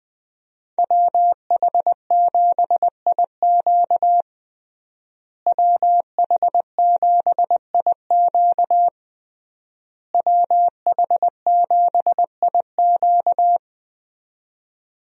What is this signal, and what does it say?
Morse code, heard as a keyed tone, about 20 wpm, 700 Hz. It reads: WH7IQ WH7IQ WH7IQ
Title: USA Call Signs - 5 Characters - Repeated 3x - 20wpm